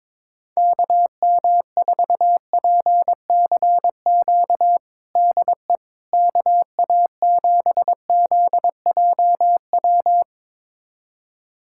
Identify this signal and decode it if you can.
Morse code, heard as a keyed tone, about 22 wpm, 700 Hz. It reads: KM4PCQ DE KA7ZJW
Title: USA Callsign DE Callsign - 6 Characters - 22wpm